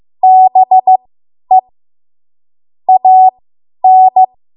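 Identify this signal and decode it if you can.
Morse code, heard as a keyed tone, about 15 wpm, spaced as for 10 wpm, 750 Hz. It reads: BE AN
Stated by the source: Morse code practice run at 10 WPM from a code practice archive.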